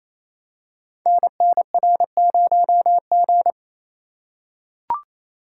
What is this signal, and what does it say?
Morse code, heard as a keyed tone, about 29 wpm, 700 Hz. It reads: NNR0G
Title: Code Groups - 5 Letter:Numbers - 28wpm